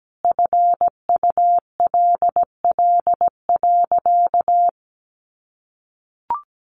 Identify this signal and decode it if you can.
Morse code, heard as a keyed tone, about 17 wpm, 700 Hz. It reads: FULL.
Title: Sentences from Top 1200 Words 17wpm